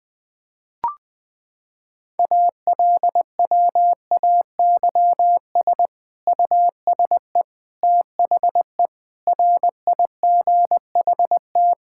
Morse code, heard as a keyed tone, about 20 wpm, 700 Hz. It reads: ALWAYS USE THE RIGHT
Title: Sentences from Top 1000 Words - Review Part 6 - 20wpm